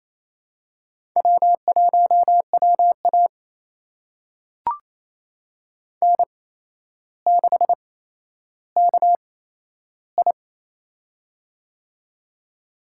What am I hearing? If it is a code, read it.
Morse code, heard as a keyed tone, about 28 wpm, spaced as for 12 wpm, 700 Hz. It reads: W1WA N6KI
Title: USA Call Signs - 4 Characters - 8x Character Spacing - 28wpm